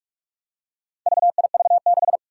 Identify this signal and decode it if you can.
Morse code, heard as a keyed tone, about 44 wpm, 700 Hz. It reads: VIV6
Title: Four Challenging Letter-Number Combinations 45wpm